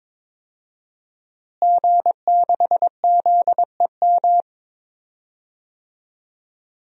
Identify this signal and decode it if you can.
Morse code, heard as a keyed tone, about 22 wpm, 700 Hz. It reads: G6ZEM
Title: DXCC-100 Call Signs - 5 Characters - Repeated 3x - 22wpm